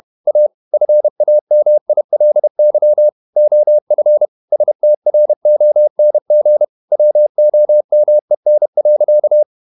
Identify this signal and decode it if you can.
Morse code, heard as a keyed tone, about 31 wpm, 600 Hz. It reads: A FAMILY OF STRONG WOMEN.
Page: Never 31wpm | CW med Gnister